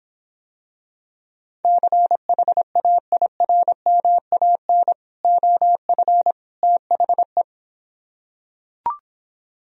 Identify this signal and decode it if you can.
Morse code, heard as a keyed tone, about 26 wpm, 700 Hz. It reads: CHAIRMAN OF THE